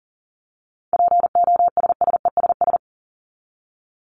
Morse code, heard as a keyed tone, about 40 wpm, 700 Hz. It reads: POSSESS
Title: Top 1751-2000 Words 40wpm